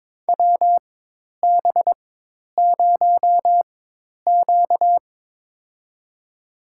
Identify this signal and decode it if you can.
Morse code, heard as a keyed tone, about 22 wpm, spaced as for 10 wpm, 700 Hz. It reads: WB0Q
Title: USA Call Signs - 4 Characters - 4x Character Spacing - 22wpm